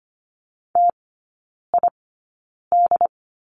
Morse code, heard as a keyed tone, about 25 wpm, spaced as for 9 wpm, 700 Hz. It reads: TID
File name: course / 25:10wpm / 093 Lesson 16e - Letter c - Three Characters 25wpm